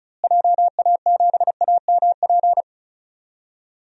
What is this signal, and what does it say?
Morse code, heard as a keyed tone, about 36 wpm, 700 Hz. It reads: JA7AMP
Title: DXCC-100 Call Signs - 6 Characters - Repeated 3x - 35wpm